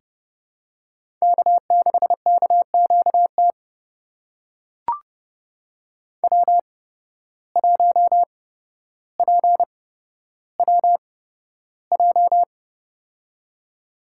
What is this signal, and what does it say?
Morse code, heard as a keyed tone, about 30 wpm, spaced as for 13 wpm, 700 Hz. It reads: K6KQT W1PWJ
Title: USA Call Signs - 5 Characters - 8x Character Spacing - 30wpm